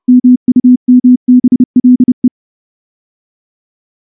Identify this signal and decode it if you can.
Morse code, heard as a keyed tone, about 29 wpm, 250 Hz. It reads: MUMBLE
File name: morse / morse-training-30WPM